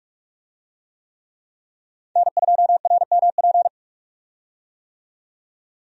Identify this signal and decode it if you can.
Morse code, heard as a keyed tone, about 46 wpm, 700 Hz. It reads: N2RMP
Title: USA Call Signs - 5 Characters - 45wpm